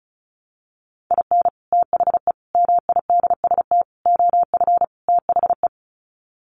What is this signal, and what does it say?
Morse code, heard as a keyed tone, about 34 wpm, 700 Hz. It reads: IN THE MIDST OF THE
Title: Sets of 5 Words 35wpm